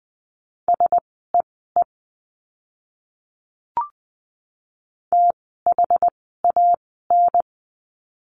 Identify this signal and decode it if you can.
Morse code, heard as a keyed tone, about 20 wpm, spaced as for 14 wpm, 700 Hz. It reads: SEE THAN
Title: 068 Lesson 12f - Letter d - Words 20wpm